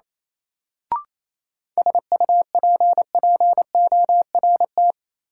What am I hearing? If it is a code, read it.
Morse code, heard as a keyed tone, about 27 wpm, 700 Hz. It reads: SUPPORT
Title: Sentences from Top 800 Words - New Word Drill - 28wpm